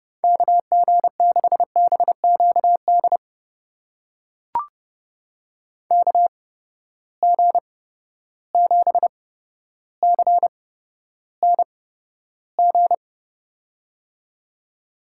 Morse code, heard as a keyed tone, about 30 wpm, spaced as for 13 wpm, 700 Hz. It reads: KG6BQD KG7CNG
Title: USA Call Signs - 6 Characters - 8x Character Spacing - 30wpm